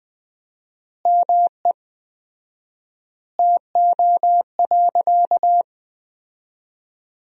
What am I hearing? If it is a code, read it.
Morse code, heard as a keyed tone, about 20 wpm, 700 Hz. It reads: ME TO.
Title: 4x Word-Spacing - Sentences from Top 400 Words - 20wpm